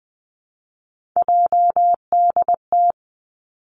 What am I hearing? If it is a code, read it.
Morse code, heard as a keyed tone, about 19 wpm, 700 Hz. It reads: JDT
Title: Three Challenging Letter-Number Combinations 20wpm